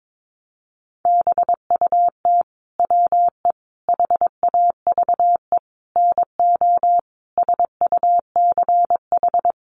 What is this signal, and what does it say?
Morse code, heard as a keyed tone, about 22 wpm, 700 Hz. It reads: BUT WE HAVE NO SUCH
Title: Sentences from Top 1300 Words 22wpm